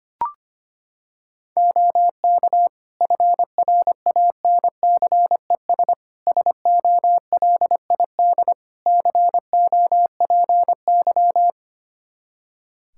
Morse code, heard as a keyed tone, about 25 wpm, 700 Hz. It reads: OK FRANCES SOLID COPY
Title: Practice QSO - Exchange 3 - More Info - 25wpm